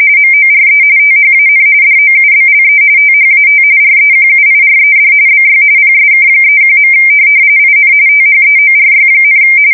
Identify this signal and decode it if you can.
RTTY, about 45.45 baud. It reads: MORE THAN DOUBLE THAT REQUIRED BY PSK31. IN THEORY,
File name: fsk170 / teletype-RTTY